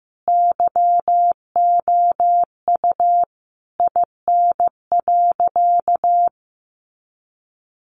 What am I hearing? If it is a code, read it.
Morse code, heard as a keyed tone, about 15 wpm, 700 Hz. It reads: YOU IN.
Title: Sentences from Top 1500 Words 15wpm